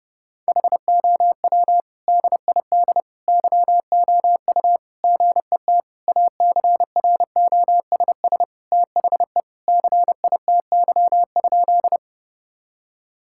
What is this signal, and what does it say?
Morse code, heard as a keyed tone, about 32 wpm, 700 Hz. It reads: HOW DID YOU GET ACROSS THE CITY?
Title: Sentences from Top 1000 Words - Review Part 5 - 30wpm